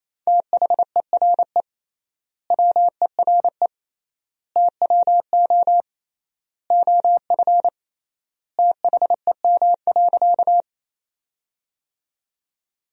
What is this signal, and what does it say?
Morse code, heard as a keyed tone, about 28 wpm, 700 Hz. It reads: THERE WERE TWO OF THEM.
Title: 3x Word-Spacing - Sentences from Top 100 Words - 28wpm